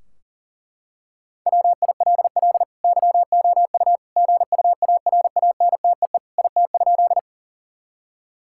Morse code, heard as a keyed tone, about 40 wpm, 700 Hz. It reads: WILL YOU GUARANTEE IT?
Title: Sentences from Top 1600 Words 40wpm